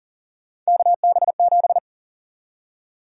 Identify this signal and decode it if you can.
Morse code, heard as a keyed tone, about 40 wpm, 700 Hz. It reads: KB7
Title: USA Call Sign Prefixes - 40wpm